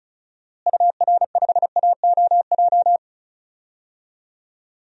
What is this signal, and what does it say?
Morse code, heard as a keyed tone, about 34 wpm, 700 Hz. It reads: UR5AOJ
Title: DXCC-100 Call Signs - 6 Characters - Repeated 2x - 35wpm